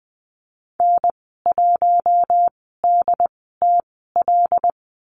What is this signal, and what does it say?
Morse code, heard as a keyed tone, about 20 wpm, spaced as for 15 wpm, 700 Hz. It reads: N1DTL
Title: 122 Lesson 20g - Number 3 - Call Signs 20wpm